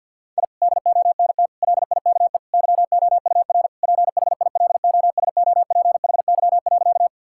Morse code, heard as a keyed tone, about 50 wpm, 700 Hz. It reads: I DONT LIKE YOUR PHILOSOPHY.
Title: Sentences from Top 1000 Words - Review Part 6 - 50wpm